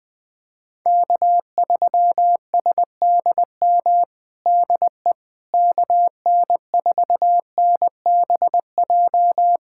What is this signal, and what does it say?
Morse code, heard as a keyed tone, about 20 wpm, 700 Hz. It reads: K3SDM DE KN4NBJ
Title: POTA QSO Practice - 20wpm